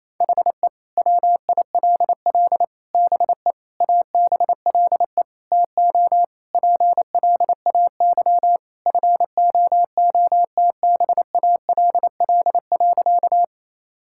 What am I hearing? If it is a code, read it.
Morse code, heard as a keyed tone, about 28 wpm, 700 Hz. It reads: HE WILL BE ABLE TO PLAY FOOTBALL.
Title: Sentences from Top 1600 Words 28wpm